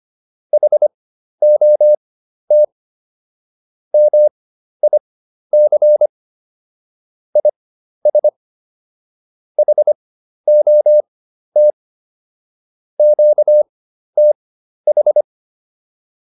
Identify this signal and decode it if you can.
Morse code, heard as a keyed tone, about 25 wpm, spaced as for 12 wpm, 600 Hz. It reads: HOT MIC IS HOT QTH